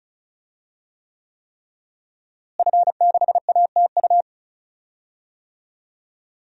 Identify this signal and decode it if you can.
Morse code, heard as a keyed tone, about 36 wpm, 700 Hz. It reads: F6ATU
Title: DXCC-100 Call Signs - 5 Characters - Repeated 3x - 35wpm